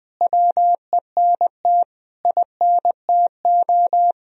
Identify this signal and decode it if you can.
Morse code, heard as a keyed tone, about 20 wpm, 700 Hz. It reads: WENT INTO
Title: Sentences from Top 1000 Words - Review Part 2 - 20wpm